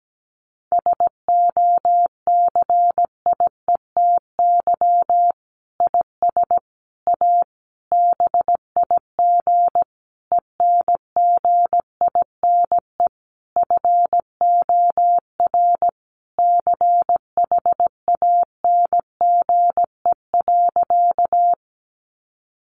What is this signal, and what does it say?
Morse code, heard as a keyed tone, about 17 wpm, 700 Hz. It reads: SOCIETY IS A BIG ENGINE FOR CHANGE.
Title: Sentences from Top 1500 Words 17wpm